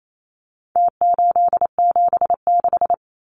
Morse code, heard as a keyed tone, about 28 wpm, 700 Hz. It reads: T876
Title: Numbers - Contest Style - 4-Digit 28wpm